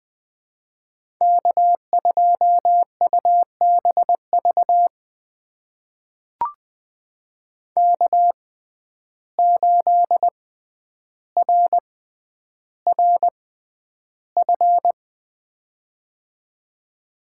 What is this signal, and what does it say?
Morse code, heard as a keyed tone, about 20 wpm, spaced as for 10 wpm, 700 Hz. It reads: K2UBV K8RRF
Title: USA Call Signs - 5 Characters - 6x Character Spacing - 20wpm